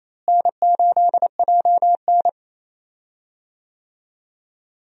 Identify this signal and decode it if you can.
Morse code, heard as a keyed tone, about 28 wpm, 700 Hz. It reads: N8JN
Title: USA Call Signs - 4 Characters - Repeated 2x - 28wpm